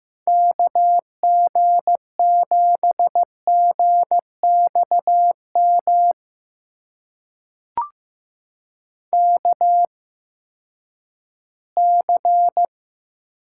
Morse code, heard as a keyed tone, about 15 wpm, spaced as for 10 wpm, 700 Hz. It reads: KG7GXM KC
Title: USA Call Signs - 6 Characters - 8x Character Spacing - 15wpm